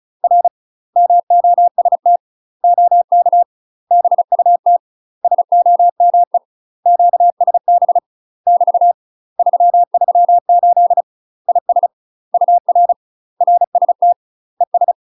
Morse code, heard as a keyed tone, about 36 wpm, 700 Hz. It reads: R MOST OK BUT SOME QSB <BT> 338 IS UR RST ES